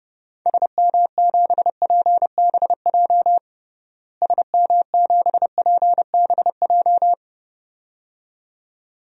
Morse code, heard as a keyed tone, about 30 wpm, 700 Hz. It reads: SM7PBJ SM7PBJ
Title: DXCC-100 Call Signs - 6 Characters - Repeated 2x - 30wpm